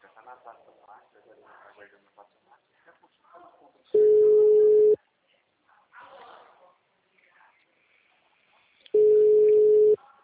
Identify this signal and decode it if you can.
Ringback tone (Europe, 425 Hz), with speech.
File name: out- -2013- - - .52277